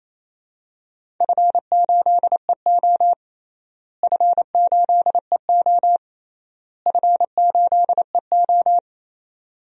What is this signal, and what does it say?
Morse code, heard as a keyed tone, about 28 wpm, 700 Hz. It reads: F8EO F8EO F8EO
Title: DXCC-100 Call Signs - 4 Characters - Repeated 3x - 28wpm